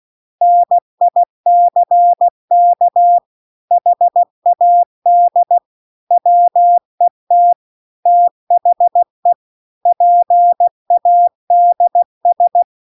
Morse code, heard as a keyed tone, about 16 wpm, 700 Hz. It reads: NICK HAD WET THE PADS
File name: CW-The-Big-Twoheaded-River-2-16WPM